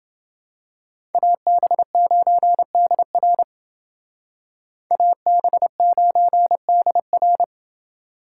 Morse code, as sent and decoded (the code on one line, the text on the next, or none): .- -... ----. -.. .-. / .- -... ----. -.. .-.
AB9DR AB9DR